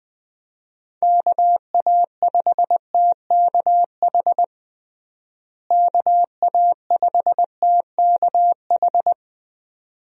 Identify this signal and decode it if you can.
Morse code, heard as a keyed tone, about 20 wpm, 700 Hz. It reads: KA5TKH KA5TKH
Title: USA Call Signs - 6 Characters - Repeated 2x - 20wpm